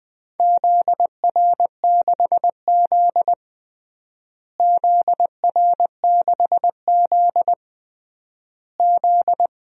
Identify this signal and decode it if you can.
Morse code, heard as a keyed tone, about 20 wpm, 700 Hz. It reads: ZR6Z ZR6Z Z
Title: DXCC-100 Call Signs - 4 Characters - Repeated 3x - 20wpm